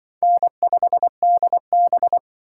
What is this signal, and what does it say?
Morse code, heard as a keyed tone, about 25 wpm, 690 Hz. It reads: N5DB
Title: Speed Racing - USA 1x2 Call Signs 20wpm